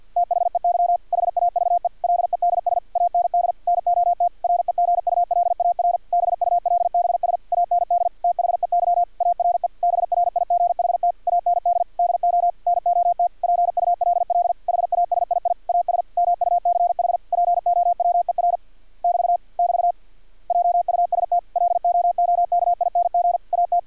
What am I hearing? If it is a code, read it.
Morse code, heard as a keyed tone, about 50 wpm, 700 Hz. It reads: THEY HAVE LEDS AND NOT REGULAR BULBS AND THEY ARE BRIGHT AND DO NOT PULL HASII AS MUCH POWER <BT> <BT> JUST LOOKING AT
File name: archived / CRSnet